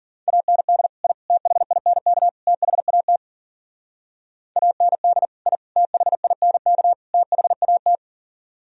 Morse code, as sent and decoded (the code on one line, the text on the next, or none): .- -. -.. / .. / - .... .. -. -.- / - .... .- - / .- -. -.. / .. / - .... .. -. -.- / - .... .- -
AND I THINK THAT AND I THINK THAT